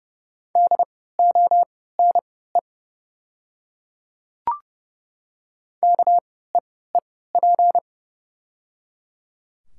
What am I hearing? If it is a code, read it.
Morse code, heard as a keyed tone, about 30 wpm, spaced as for 17 wpm, 700 Hz. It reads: DONE KEEP
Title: 3x Character Spacing - Top 500 Words - 30wpm